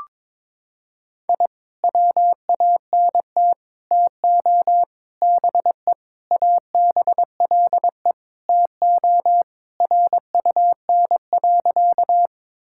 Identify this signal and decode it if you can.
Morse code, heard as a keyed tone, about 22 wpm, 700 Hz. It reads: I WANT TO BE ABLE TO RUN.
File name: practice / 22wpm / Sentences with 3-Word Phrases - 22wpm